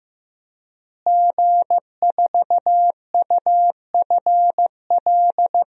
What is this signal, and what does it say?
Morse code, heard as a keyed tone, about 15 wpm, 700 Hz. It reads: G4UFL
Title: DXCC-100 Call Signs - 5 Characters - Repeated 3x - 15wpm